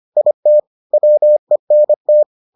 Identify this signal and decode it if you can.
Morse code, heard as a keyed tone, about 25 wpm, 600 Hz. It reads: IT WENT